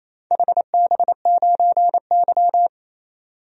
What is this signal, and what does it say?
Morse code, heard as a keyed tone, about 28 wpm, 690 Hz. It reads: HB9Y
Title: DXCC-100 Call Signs - 4 Characters - 28wpm